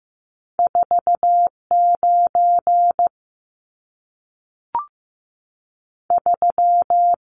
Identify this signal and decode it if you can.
Morse code, heard as a keyed tone, about 15 wpm, spaced as for 3 wpm, 700 Hz. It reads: 4 93
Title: Numbers - 2-Digit 15wpm